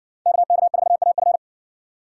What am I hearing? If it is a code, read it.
Morse code, heard as a keyed tone, about 60 wpm, 700 Hz. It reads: KK4AF